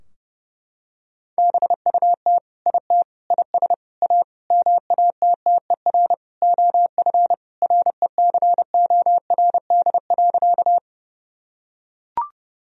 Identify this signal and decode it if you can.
Morse code, heard as a keyed tone, about 30 wpm, 700 Hz. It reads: BUT IT IS A MATTER OF RECORD.
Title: Sentences from Top 1000 Words - Review Part 3 - 30wpm